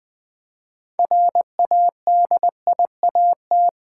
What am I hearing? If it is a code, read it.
Morse code, heard as a keyed tone, about 20 wpm, 700 Hz. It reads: RADIAT
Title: Build A Word - 3 Letters At A Time - 9-Letter Words - 20wpm